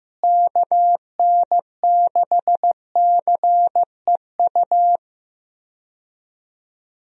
Morse code, heard as a keyed tone, about 15 wpm, 700 Hz. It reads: KN6CEU
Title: USA Call Signs - 6 Characters - Repeated 2x - 15wpm